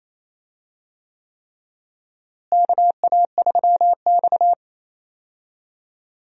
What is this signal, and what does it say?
Morse code, heard as a keyed tone, about 28 wpm, 700 Hz. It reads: KA3X
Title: USA Call Signs - 4 Characters - Repeated 3x - 28wpm